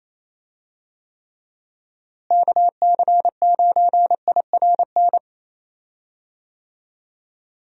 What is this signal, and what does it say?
Morse code, heard as a keyed tone, about 28 wpm, 700 Hz. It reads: KC9IRN
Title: USA Call Signs - 6 Characters - Repeated 3x - 28wpm